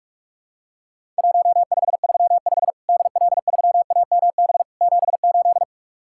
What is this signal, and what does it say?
Morse code, heard as a keyed tone, about 44 wpm, 700 Hz. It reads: 1535 DL3AMB 78
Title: Worked All Europe DX Contest - QTC Exchange - 45wpm